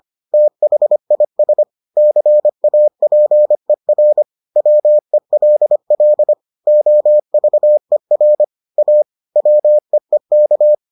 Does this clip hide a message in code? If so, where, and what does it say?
Yes: Morse code from 0.3 to 10.7 s, heard as a keyed tone, about 25 wpm, 600 Hz. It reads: THIS CAPER WELL OVER A WEEK